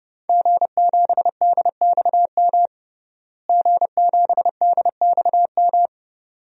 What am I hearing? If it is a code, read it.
Morse code, heard as a keyed tone, about 30 wpm, 700 Hz. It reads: G7DXM G7DXM
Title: DXCC-100 Call Signs - 5 Characters - Repeated 2x - 30wpm